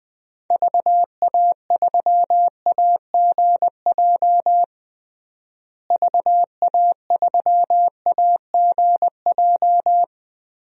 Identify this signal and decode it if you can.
Morse code, heard as a keyed tone, about 19 wpm, 700 Hz. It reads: VA3AGJ VA3AGJ
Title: DXCC-100 Call Signs - 6 Characters - Repeated 3x - 20wpm